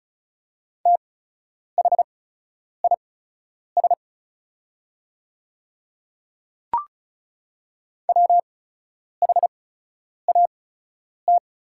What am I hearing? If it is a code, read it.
Morse code, heard as a keyed tone, about 34 wpm, spaced as for 10 wpm, 690 Hz. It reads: THIS WHAT
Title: 8x Character Spacing - Top 100 Words - 35wpm